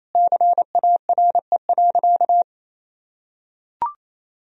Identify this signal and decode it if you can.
Morse code, heard as a keyed tone, about 28 wpm, 700 Hz. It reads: CARE.
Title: Sentences from Top 1000 Words - Review Part 3 - 28wpm